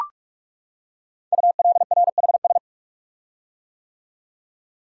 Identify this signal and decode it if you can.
Morse code, heard as a keyed tone, about 44 wpm, 700 Hz. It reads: ULRHS
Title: Code Groups - 5 Letters - 45wpm